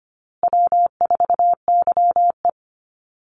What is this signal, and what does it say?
Morse code, heard as a keyed tone, about 25 wpm, 700 Hz. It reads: W4YE
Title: Easy Call Signs Repeated 2x - 25wpm